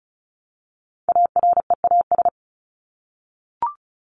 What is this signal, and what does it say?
Morse code, heard as a keyed tone, about 34 wpm, 700 Hz. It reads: AREAS
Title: ES:S Suffix - Top 100 Words with Base Words 35wpm